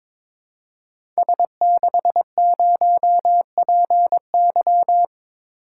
Morse code, heard as a keyed tone, about 22 wpm, 700 Hz. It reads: S60PY
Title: Code Groups - 5 Letter:Numbers - 22wpm